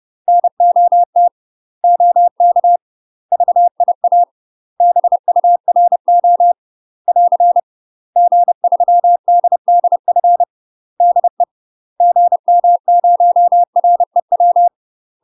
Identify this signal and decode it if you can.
Morse code, heard as a keyed tone, about 30 wpm, 700 Hz. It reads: NOT OK VIA BURO <AR> G3DDF DE GM0REW
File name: UK MASTER QSOs TRAINING / 30WPM / UK MASTER QSOs 0008